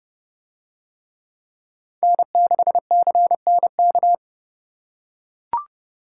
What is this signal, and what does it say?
Morse code, heard as a keyed tone, about 28 wpm, 700 Hz. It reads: N6CNK